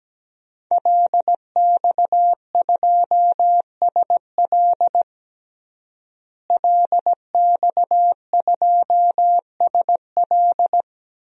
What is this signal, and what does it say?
Morse code, heard as a keyed tone, about 17 wpm, 700 Hz. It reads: LX2SL LX2SL
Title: DXCC-100 Call Signs - 5 Characters - Repeated 3x - 17wpm